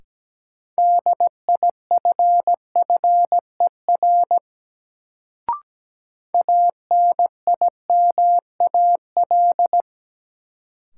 Morse code, heard as a keyed tone, about 17 wpm, 700 Hz. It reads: DIFFER ANIMAL
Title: Sentences from Top 1000 Words - Review Part 1 - New Word Drill - 17wpm